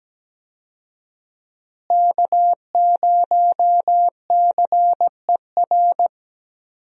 Morse code, heard as a keyed tone, about 17 wpm, 700 Hz. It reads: K0CER
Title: USA Call Signs - 5 Characters - Repeated 2x - 17wpm